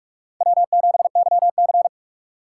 Morse code, heard as a keyed tone, about 46 wpm, 700 Hz. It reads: W7YC
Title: USA Call Signs - 4 Characters - 4x Character Spacing - 45wpm